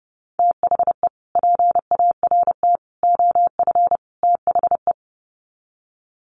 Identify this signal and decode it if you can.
Morse code, heard as a keyed tone, about 28 wpm, 700 Hz. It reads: THE PART OF THE
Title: Sets of 4 Words 30wpm